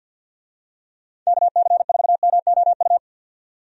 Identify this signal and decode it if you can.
Morse code, heard as a keyed tone, about 48 wpm, 700 Hz. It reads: KC4MOU
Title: USA Call Signs - 6 Characters - 50wpm